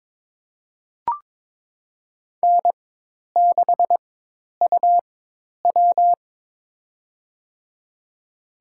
Morse code, heard as a keyed tone, about 22 wpm, spaced as for 10 wpm, 700 Hz. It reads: N6UW
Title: USA Call Signs - 4 Characters - 4x Character Spacing - 22wpm